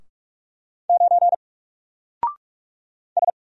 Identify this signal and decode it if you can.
Morse code, heard as a keyed tone, about 46 wpm, 700 Hz. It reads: 9 S
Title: Single Letter-Number 45wpm